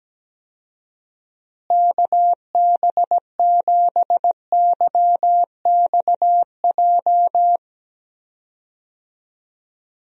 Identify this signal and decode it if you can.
Morse code, heard as a keyed tone, about 17 wpm, 700 Hz. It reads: KB7YXJ
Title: USA Call Signs - 6 Characters - Repeated 2x - 17wpm